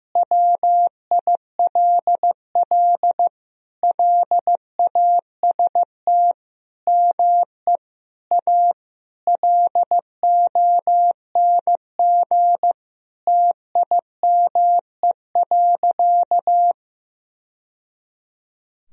Morse code, heard as a keyed tone, about 15 wpm, 700 Hz. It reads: WILL LAST ME A LONG TIME.
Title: Sentences from Top 1000 Words - Review Part 2 - 15wpm